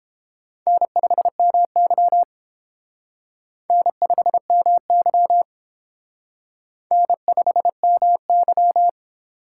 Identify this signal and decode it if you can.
Morse code, heard as a keyed tone, about 29 wpm, 700 Hz. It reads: N5MY N5MY N5MY